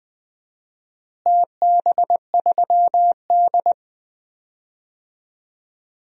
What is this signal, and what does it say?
Morse code, heard as a keyed tone, about 20 wpm, 700 Hz. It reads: TB3D
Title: DXCC-100 Call Signs - 4 Characters - Repeated 2x - 20wpm